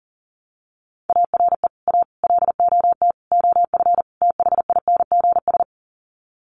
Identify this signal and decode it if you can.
Morse code, heard as a keyed tone, about 40 wpm, 700 Hz. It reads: ARE A LOT OF THINGS
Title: Sets of 5 Words Encore2 40wpm